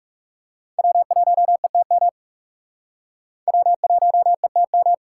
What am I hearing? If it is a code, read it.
Morse code, heard as a keyed tone, about 38 wpm, 700 Hz. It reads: W1ETM W1ETM
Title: Speed Racing - USA 1x3 Call Signs 30wpm